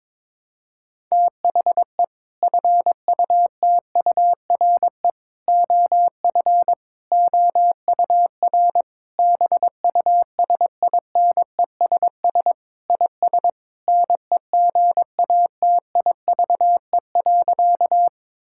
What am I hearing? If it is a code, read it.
Morse code, heard as a keyed tone, about 22 wpm, 700 Hz. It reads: THE FUTURE OF OUR BUSINESS IS NEGATIVE.
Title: Sentences from Top 1000 Words - Review Part 6 - 22wpm